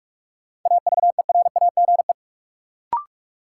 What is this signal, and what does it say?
Morse code, heard as a keyed tone, about 41 wpm, 700 Hz. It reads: AVERAGE
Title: Sets of 5 Words - New Word Drill - 45wpm